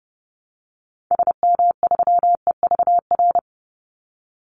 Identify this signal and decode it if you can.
Morse code, heard as a keyed tone, about 30 wpm, 700 Hz. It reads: SM3EVR
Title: Call Signs Repeated 2x - 30wpm